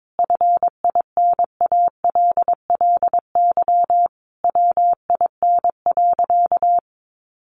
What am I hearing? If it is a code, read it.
Morse code, heard as a keyed tone, about 22 wpm, 700 Hz. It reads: FINALLY WIN.
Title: Sentences from Top 1500 Words 22wpm